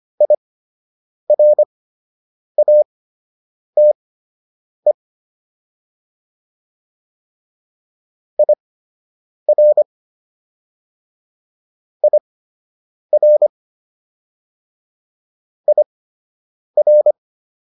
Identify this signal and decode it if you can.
Morse code, heard as a keyed tone, about 25 wpm, spaced as for 8 wpm, 600 Hz. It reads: IRATE IR IR IR